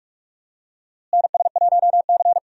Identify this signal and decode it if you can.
Morse code, heard as a keyed tone, about 46 wpm, 700 Hz. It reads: NS1C
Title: USA Call Signs - 4 Characters - Repeated 3x - 45wpm